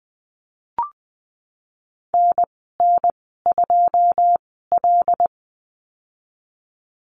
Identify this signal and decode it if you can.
Morse code, heard as a keyed tone, about 20 wpm, spaced as for 15 wpm, 700 Hz. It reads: NN2L
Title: 082 Lesson 14g - Number 2 - Call Signs 20wpm